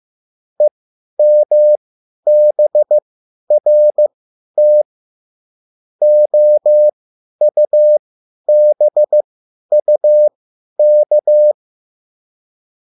Morse code, heard as a keyed tone, about 15 wpm, spaced as for 10 wpm, 600 Hz. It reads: EMBRT OUBUK